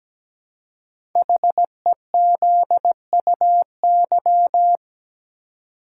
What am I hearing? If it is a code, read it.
Morse code, heard as a keyed tone, about 17 wpm, 700 Hz. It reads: HEZUY
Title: Code Groups - 5 Letters - 17wpm